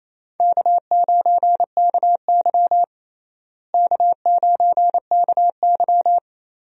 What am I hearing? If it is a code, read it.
Morse code, heard as a keyed tone, about 28 wpm, 700 Hz. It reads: K9KY K9KY
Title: USA Call Signs - 4 Characters - Repeated 2x - 28wpm